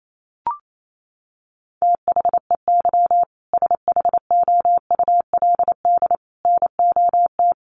Morse code, heard as a keyed tone, about 28 wpm, 700 Hz. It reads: THEY SHOULD NOT
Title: Sentences from Top 1500 Words 28wpm